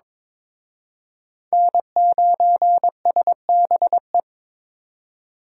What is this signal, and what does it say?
Morse code, heard as a keyed tone, about 26 wpm, 700 Hz. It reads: N9SBE
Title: Speed Racing - USA 1x3 Call Signs 22wpm